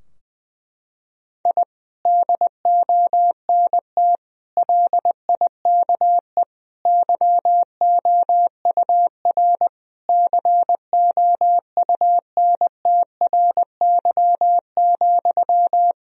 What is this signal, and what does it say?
Morse code, heard as a keyed tone, about 20 wpm, 700 Hz. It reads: I DONT LIKE YOUR COUNTRY,
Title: Sentences from Top 1000 Words - Review Part 3 - 20wpm